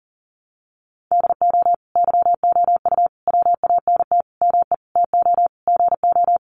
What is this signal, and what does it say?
Morse code, heard as a keyed tone, about 40 wpm, 700 Hz. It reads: DO YOU WANT ME TO GO
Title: Sentences from Top 600 Words 40wpm